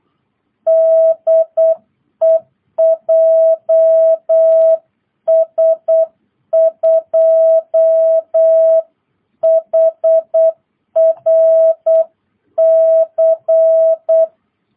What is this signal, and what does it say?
Morse code, heard as a keyed tone, about 8 wpm, 660 Hz. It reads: DE JS2HRC